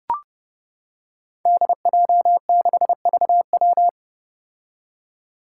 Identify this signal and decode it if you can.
Morse code, heard as a keyed tone, about 30 wpm, 700 Hz. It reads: DJ6VW
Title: Five Challenging Letter-Number Combinations 30wpm